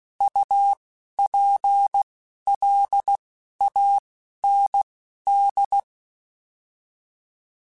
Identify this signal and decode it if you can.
Morse code, heard as a keyed tone, about 16 wpm, spaced as for 11 wpm, 800 Hz. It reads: UPLAND